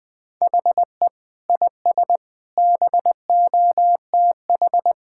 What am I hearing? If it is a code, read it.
Morse code, heard as a keyed tone, about 21 wpm, 700 Hz. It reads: HE IS BOTH
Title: Sentences from Top 1600 Words 20wpm